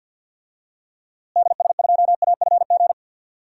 Morse code, heard as a keyed tone, about 50 wpm, 700 Hz. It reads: DS2AFG
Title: DXCC-100 Call Signs - 6 Characters - Repeated 2x - 50wpm